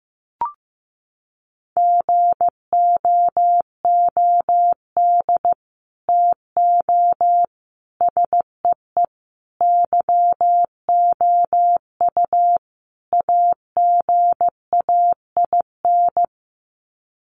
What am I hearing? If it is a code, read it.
Morse code, heard as a keyed tone, about 15 wpm, 700 Hz. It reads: GOOD TO SEE YOU AGAIN
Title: Sets of 5 Words Encore2 15wpm